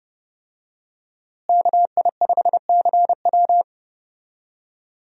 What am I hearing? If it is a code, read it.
Morse code, heard as a keyed tone, about 30 wpm, 700 Hz. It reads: KI5CW
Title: USA Call Signs - 5 Characters - Repeated 2x - 30wpm